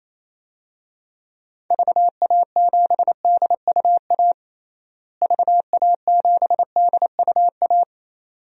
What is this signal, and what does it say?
Morse code, heard as a keyed tone, about 28 wpm, 700 Hz. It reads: VA7DUA VA7DUA
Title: DXCC-100 Call Signs - 6 Characters - Repeated 2x - 28wpm